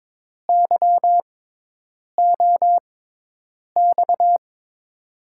Morse code, heard as a keyed tone, about 22 wpm, spaced as for 8 wpm, 700 Hz. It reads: YOX
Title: USA Call Signs - 5 Characters - 6x Character Spacing - 22wpm